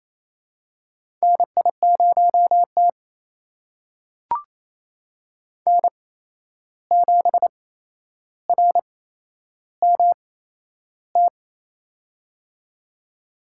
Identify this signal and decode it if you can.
Morse code, heard as a keyed tone, about 28 wpm, spaced as for 11 wpm, 700 Hz. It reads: NI0T N7RMT
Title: USA Call Signs - 8x Character Spacing - 28wpm